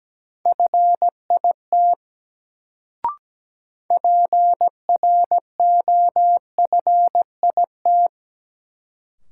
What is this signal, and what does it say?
Morse code, heard as a keyed tone, about 17 wpm, 700 Hz. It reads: FIT PROFIT
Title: Sentences from Top 800 Words - New Word Drill - 17wpm